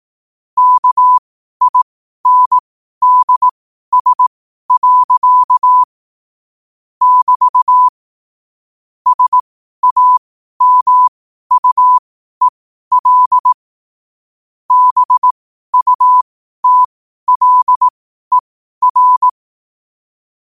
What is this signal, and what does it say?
Morse code, heard as a keyed tone, about 18 wpm, spaced as for 12 wpm, 1000 Hz. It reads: KINDS. <BT> SAMUEL BUTLER
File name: qotdfiles / 12wpm